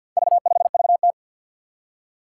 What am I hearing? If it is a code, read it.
Morse code, heard as a keyed tone, about 52 wpm, 700 Hz. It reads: V5VT
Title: Four Challenging Letter-Number Combinations 50wpm